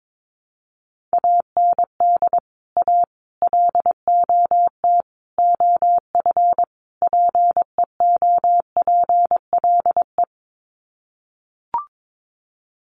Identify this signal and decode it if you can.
Morse code, heard as a keyed tone, about 22 wpm, 700 Hz. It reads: AND A LOT OF PEOPLE
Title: Sets of 5 Words 22wpm